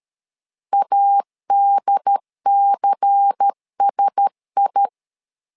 Morse code, heard as a keyed tone, about 13 wpm, 800 Hz. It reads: ADCSI
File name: LEZIONI / 5 LBVC / 5 LBVC 12wpm 1955